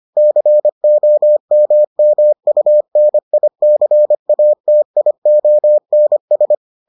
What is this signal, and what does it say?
Morse code, heard as a keyed tone, about 25 wpm, 600 Hz. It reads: COMMUNICATIONS